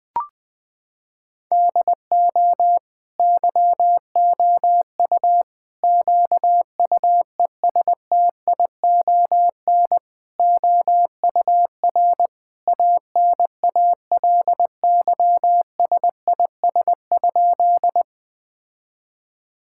Morse code, heard as a keyed tone, about 20 wpm, 700 Hz. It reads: DO YOU QUESTION OUR ANALYSIS?
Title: Sentences from Top 1000 Words - Review Part 5 - 20wpm